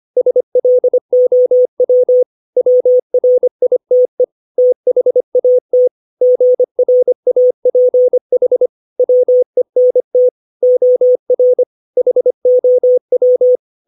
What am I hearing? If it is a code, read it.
Morse code, heard as a keyed tone, about 25 wpm, 500 Hz. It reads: SLOW WRITE THAT GRAPH WENT OR HOW
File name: 632 words / 632 Words 25WPM